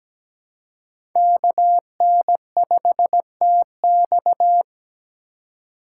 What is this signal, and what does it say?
Morse code, heard as a keyed tone, about 17 wpm, 700 Hz. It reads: KN5TX
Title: K1USN SST QSO Practice - 17wpm